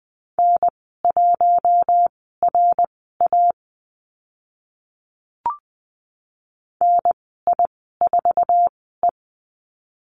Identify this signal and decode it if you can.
Morse code, heard as a keyed tone, about 20 wpm, spaced as for 14 wpm, 700 Hz. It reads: N1RA NI4E
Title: 062 Lesson 11g - Letter h - Call Signs 20wpm